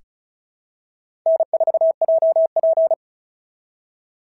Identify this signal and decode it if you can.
Morse code, heard as a keyed tone, about 34 wpm, 650 Hz. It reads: N4JP
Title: SOTA QSO Practice 35wpm